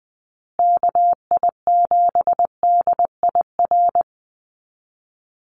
Morse code, heard as a keyed tone, about 20 wpm, 700 Hz. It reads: KI7DIR
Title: USA Call Signs Repeated 3x 20wpm